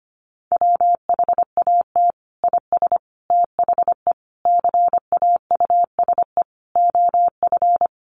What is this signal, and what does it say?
Morse code, heard as a keyed tone, about 25 wpm, 700 Hz. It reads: WHAT IS THE CAUSE OF
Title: Sentences from Top 1100 Words 25wpm